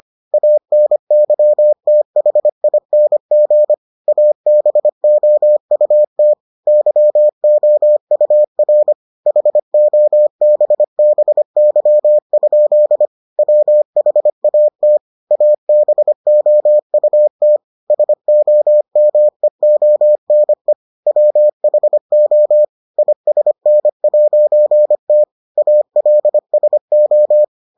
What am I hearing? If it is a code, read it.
Morse code, heard as a keyed tone, about 25 wpm, 600 Hz. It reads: ANYTHING ABOUT YOUR HOBBY? WHAT ABOUT SOMEONE WHO ISN'T ALSO